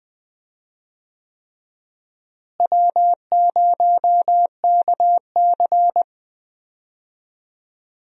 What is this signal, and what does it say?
Morse code, heard as a keyed tone, about 21 wpm, 700 Hz. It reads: W0KC
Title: USA Call Signs - 4 Characters - Repeated 3x - 20wpm